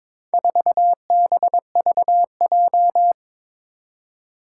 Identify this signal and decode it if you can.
Morse code, heard as a keyed tone, about 22 wpm, 700 Hz. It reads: 4BVJ
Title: Four Challenging Letter-Number Combinations 22wpm